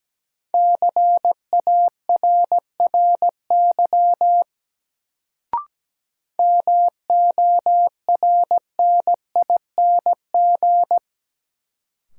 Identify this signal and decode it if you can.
Morse code, heard as a keyed tone, about 17 wpm, 700 Hz. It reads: CARRY MORNING
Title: Sentences from Top 400 Words - New Word Drill - 17wpm